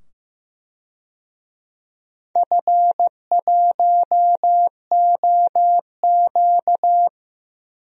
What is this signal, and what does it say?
Morse code, heard as a keyed tone, about 15 wpm, 700 Hz. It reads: F1OQ
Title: DXCC-100 Call Signs - 4 Characters - 15wpm